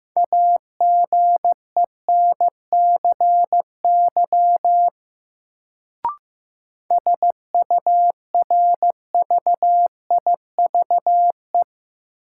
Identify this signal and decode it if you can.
Morse code, heard as a keyed tone, about 16 wpm, 700 Hz. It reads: AGENCY SURVIVE
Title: Sentences from Top 1300 Words - New Word Drill - 15wpm